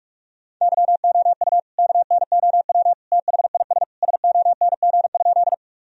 Morse code, heard as a keyed tone, about 46 wpm, 700 Hz. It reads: YOU KNOW THIS SONG?
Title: Sentences from Top 1000 Words - Review Part 2 - 45wpm